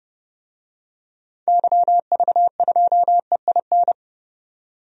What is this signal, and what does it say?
Morse code, heard as a keyed tone, about 30 wpm, 700 Hz. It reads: YV2EIN
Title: DXCC-100 Call Signs - 6 Characters - 30wpm